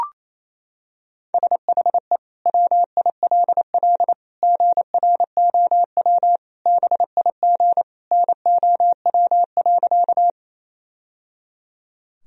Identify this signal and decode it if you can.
Morse code, heard as a keyed tone, about 28 wpm, 700 Hz. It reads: SHE WILL GROW BIG NOW.
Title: Sentences from Top 1000 Words - Review Part 1 - 28wpm